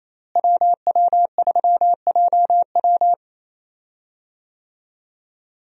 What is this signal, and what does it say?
Morse code, heard as a keyed tone, about 27 wpm, 700 Hz. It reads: WW3JW
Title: USA Call Signs - 5 Characters - Repeated 3x - 28wpm